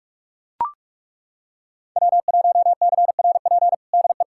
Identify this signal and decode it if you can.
Morse code, heard as a keyed tone, about 41 wpm, 700 Hz. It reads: W1CRP DE
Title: USA Callsign DE Callsign - 5 Characters - 45wpm